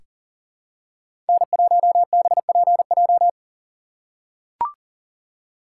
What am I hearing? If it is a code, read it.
Morse code, heard as a keyed tone, about 40 wpm, 700 Hz. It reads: N1BPJ
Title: USA Call Signs - 5 Characters - 6x Character Spacing - 40wpm